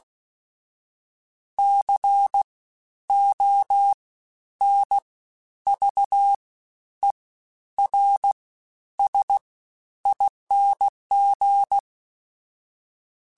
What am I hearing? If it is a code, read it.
Morse code, heard as a keyed tone, about 16 wpm, 800 Hz. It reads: C O N V E R S ING
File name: Special INGs-mp3s / Special INGs-mp3s / SlowToFaster ing @15